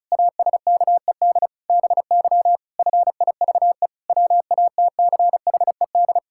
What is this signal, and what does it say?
Morse code, heard as a keyed tone, about 36 wpm, 700 Hz. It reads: ASKED BY FIVE WATCHED
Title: Pangram Challenge v2 - 35wpm